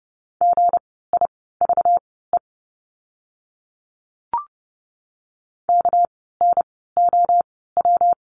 Morse code, heard as a keyed tone, about 30 wpm, spaced as for 17 wpm, 700 Hz. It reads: GIVE KNOW
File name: course / 30:10wpm / 230 Lesson 37f - Letter x - Words 30wpm